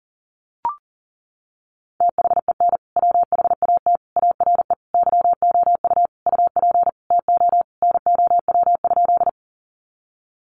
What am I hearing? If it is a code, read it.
Morse code, heard as a keyed tone, about 40 wpm, 700 Hz. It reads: THEN WHAT ARE YOU UP TO NOW?
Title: Sentences from Top 100 Words 40wpm